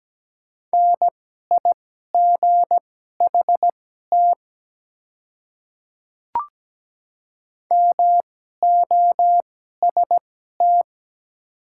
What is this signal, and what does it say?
Morse code, heard as a keyed tone, about 17 wpm, spaced as for 12 wpm, 690 Hz. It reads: NIGHT MOST
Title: 2x Character Spacing - Top 300 Words - 17wpm